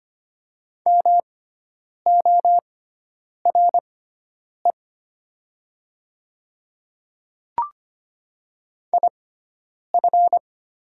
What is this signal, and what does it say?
Morse code, heard as a keyed tone, about 25 wpm, spaced as for 9 wpm, 700 Hz. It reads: MORE IF
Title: 6x Character Spacing - Top 100 Words - 25wpm